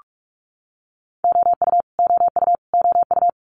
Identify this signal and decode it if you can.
Morse code, heard as a keyed tone, about 46 wpm, 700 Hz. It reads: OU OU OU
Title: Multi-letter Phonemes Repeated 3x 45wpm